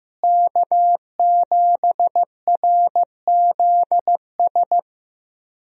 Morse code, heard as a keyed tone, about 15 wpm, 700 Hz. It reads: K7RZS
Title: USA Call Signs - 5 Characters - Repeated 3x - 15wpm